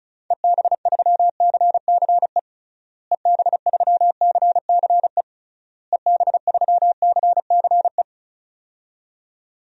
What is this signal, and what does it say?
Morse code, heard as a keyed tone, about 36 wpm, 700 Hz. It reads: EB3CCE EB3CCE EB3CCE
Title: DXCC-100 Call Signs - 6 Characters - Repeated 3x - 35wpm